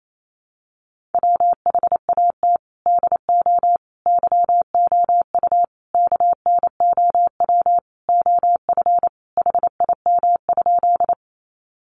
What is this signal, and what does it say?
Morse code, heard as a keyed tone, about 28 wpm, 700 Hz. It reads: WHAT DO YOU KNOW OF HIM?
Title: Sentences from Top 100 Words 28wpm